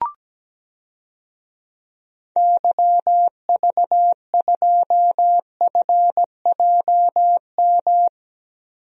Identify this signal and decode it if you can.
Morse code, heard as a keyed tone, about 17 wpm, 700 Hz. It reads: YV2FJM
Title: DXCC-100 Call Signs - 6 Characters - Repeated 2x - 17wpm